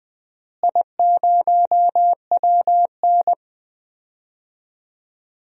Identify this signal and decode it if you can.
Morse code, heard as a keyed tone, about 20 wpm, 700 Hz. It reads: I0WN
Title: DXCC-100 Call Signs - 4 Characters - Repeated 3x - 20wpm